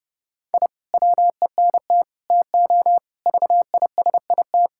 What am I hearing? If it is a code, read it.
Morse code, heard as a keyed tone, about 32 wpm, 700 Hz. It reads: I WENT TO VISIT
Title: Sentences from Top 1000 Words - Review Part 6 - 30wpm